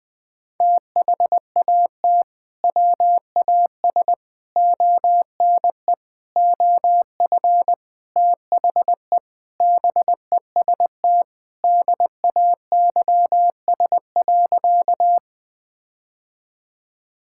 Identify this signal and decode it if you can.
Morse code, heard as a keyed tone, about 20 wpm, 700 Hz. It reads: THAT WAS ONE OF THE BEST DAYS.
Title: Sentences with 5-Word Phrases - 20wpm